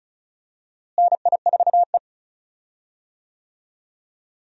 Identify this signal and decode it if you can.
Morse code, heard as a keyed tone, about 36 wpm, 700 Hz. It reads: NI4E
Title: USA Call Signs - 4 Characters - Repeated 2x - 35wpm